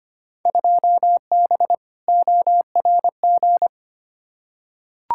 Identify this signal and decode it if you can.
Morse code, heard as a keyed tone, about 25 wpm, 700 Hz. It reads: 2B ORG
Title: Field Day Exchanges Encore - 25wpm